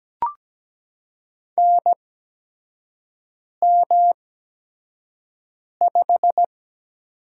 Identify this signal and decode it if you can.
Morse code, heard as a keyed tone, about 17 wpm, spaced as for 5 wpm, 700 Hz. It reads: NM5
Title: USA Call Signs - 4 Characters - 8x Character Spacing - 17wpm